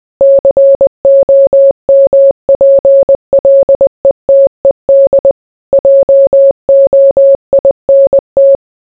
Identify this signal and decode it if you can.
Morse code, heard as a keyed tone, about 20 wpm, 550 Hz. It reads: COMPLETED JOINT